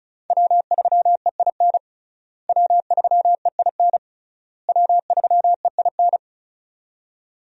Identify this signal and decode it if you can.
Morse code, heard as a keyed tone, about 36 wpm, 700 Hz. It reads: W3EIN W3EIN W3EIN
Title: USA Call Signs - 5 Characters - Repeated 3x - 35wpm